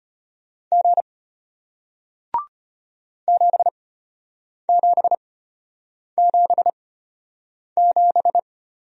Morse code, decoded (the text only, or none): G 7777